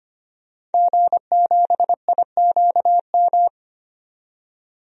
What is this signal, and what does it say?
Morse code, heard as a keyed tone, about 25 wpm, 700 Hz. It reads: G7IQM
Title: DXCC-100 Call Signs - 5 Characters - 25wpm